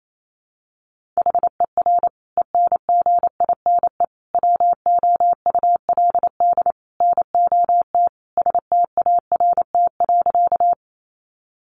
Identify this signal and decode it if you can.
Morse code, heard as a keyed tone, about 28 wpm, 700 Hz. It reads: HER ENGINE WOULD NOT START.
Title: Sentences from Top 1500 Words 28wpm